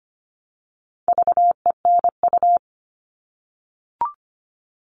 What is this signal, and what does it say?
Morse code, heard as a keyed tone, about 25 wpm, 700 Hz. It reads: VENU
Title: All Cut Numbers - 4-Digit 25wpm